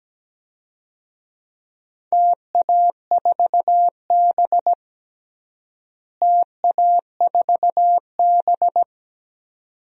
Morse code, heard as a keyed tone, about 17 wpm, 700 Hz. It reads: TA4B TA4B
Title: DXCC-100 Call Signs - 4 Characters - Repeated 2x - 17wpm